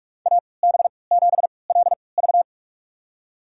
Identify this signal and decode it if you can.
Morse code, heard as a keyed tone, about 46 wpm, spaced as for 25 wpm, 700 Hz. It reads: AB7LV
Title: USA Call Signs - 5 Characters - 3x Character Spacing - 45wpm